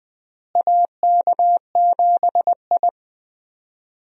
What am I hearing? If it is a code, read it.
Morse code, heard as a keyed tone, about 20 wpm, 700 Hz. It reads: AK7I
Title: USA Call Signs - 4 Characters - Repeated 2x - 20wpm